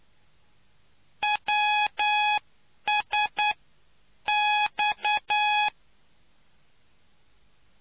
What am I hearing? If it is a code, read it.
Morse code, heard as a keyed tone, about 9 wpm, spaced as for 8 wpm, 840 Hz. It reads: WSX